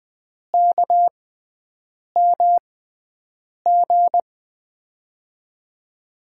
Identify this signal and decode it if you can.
Morse code, heard as a keyed tone, about 21 wpm, spaced as for 7 wpm, 700 Hz. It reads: KMG
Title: USA Call Signs - 6 Characters - 6x Character Spacing - 20wpm